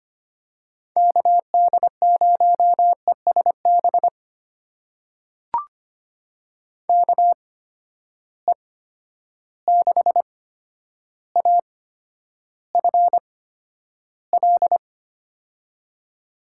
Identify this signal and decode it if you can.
Morse code, heard as a keyed tone, about 25 wpm, spaced as for 11 wpm, 700 Hz. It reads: KD0ESB KE6AFL
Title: USA Call Signs - 6 Characters - 8x Character Spacing - 25wpm